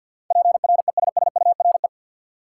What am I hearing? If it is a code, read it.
Morse code, heard as a keyed tone, about 53 wpm, 700 Hz. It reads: PRESSURE
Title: Sentences from Top 1400 Words - New Word Drill Alt - 50wpm